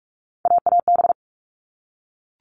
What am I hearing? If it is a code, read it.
Morse code, heard as a keyed tone, about 44 wpm, 700 Hz. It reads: AAB
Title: All Cut Numbers - 3-Digit 45wpm